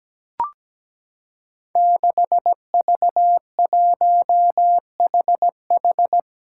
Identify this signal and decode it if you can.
Morse code, heard as a keyed tone, about 17 wpm, 700 Hz. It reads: 6V1HH
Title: Five Challenging Letter-Number Combinations 17wpm